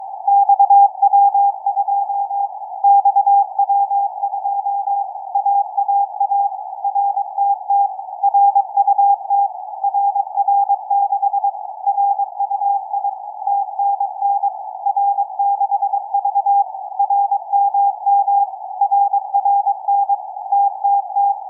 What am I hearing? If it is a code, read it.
Morse code, heard as a keyed tone, about 22 wpm, 800 Hz. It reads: XW2 XW2 AAA RTT RUT RR6 RUI TNN R6V RMM RRN TTT
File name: XW2C-CAS3C-VERT-01 16h54 utc 145mhz790 cw